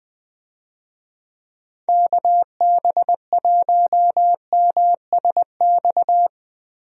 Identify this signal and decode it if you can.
Morse code, heard as a keyed tone, about 20 wpm, 700 Hz. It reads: KB1MSX
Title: USA Call Signs - 6 Characters - Repeated 3x - 20wpm